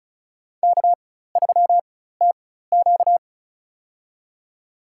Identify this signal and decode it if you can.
Morse code, heard as a keyed tone, about 34 wpm, spaced as for 16 wpm, 700 Hz. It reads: K3TQ
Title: USA Call Signs - 4 Characters - 4x Character Spacing - 35wpm